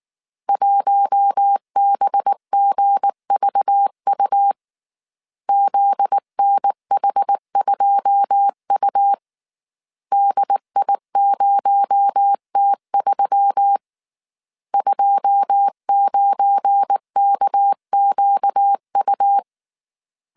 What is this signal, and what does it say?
Morse code, heard as a keyed tone, about 19 wpm, 800 Hz. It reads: 1BGVU ZNH2U DI0T3 29KQU
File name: CW / DET / 1 9 1500C S A 18wpm 2459